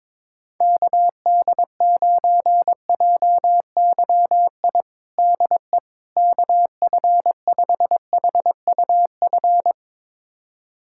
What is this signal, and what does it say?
Morse code, heard as a keyed tone, about 22 wpm, 700 Hz. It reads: KD9JYI DE KF5HUF
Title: USA Callsign DE Callsign 22wpm